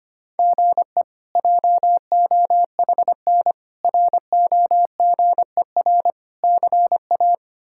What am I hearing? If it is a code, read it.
Morse code, heard as a keyed tone, about 25 wpm, 700 Hz. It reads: GE JOHN ROGER CA
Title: K1USN SST QSO Practice - 25wpm